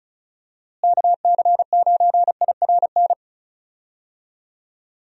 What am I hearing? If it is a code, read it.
Morse code, heard as a keyed tone, about 34 wpm, 700 Hz. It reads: KC9IRN
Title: USA Call Signs - 6 Characters - Repeated 3x - 35wpm